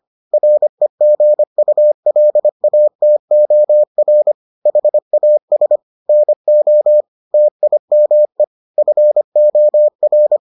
Morse code, heard as a keyed tone, about 25 wpm, 600 Hz. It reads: REGULATOR HAS NO TIME FOR